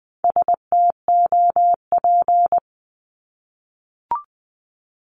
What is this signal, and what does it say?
Morse code, heard as a keyed tone, about 20 wpm, 700 Hz. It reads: STOP
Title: Multi-letter Phonemes 20wpm